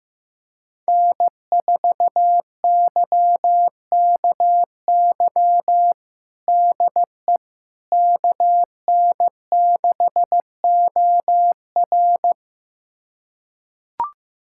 Morse code, heard as a keyed tone, about 15 wpm, 700 Hz. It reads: N4YKY DE KN6OR
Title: USA Callsign DE Callsign 15wpm